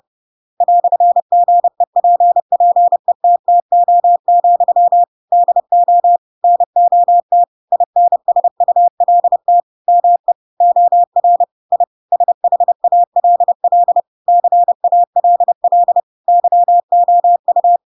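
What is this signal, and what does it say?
Morse code, heard as a keyed tone, about 30 wpm, 700 Hz. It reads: "GEPPETTO, DO NOT INSULT ME OR I SHALL CALL YOU
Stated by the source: Chapter 2 of The Adventures of Pinocchio read as a book in Morse code (CW) at 30 wpm, by ik1hge.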